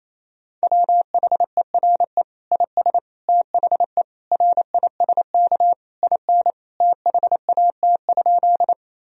Morse code, heard as a keyed tone, about 28 wpm, 700 Hz. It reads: WHERE IS THE RISK IN THAT?
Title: Sentences from Top 1000 Words - Review Part 4 - 28wpm